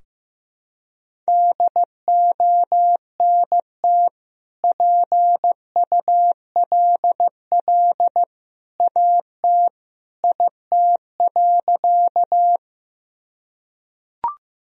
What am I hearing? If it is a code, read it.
Morse code, heard as a keyed tone, about 15 wpm, 700 Hz. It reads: DONT PULL AT IT.
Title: Sentences from Top 1000 Words - Review Part 2 - 15wpm